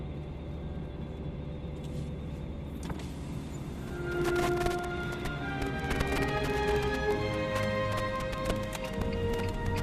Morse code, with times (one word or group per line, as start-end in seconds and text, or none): none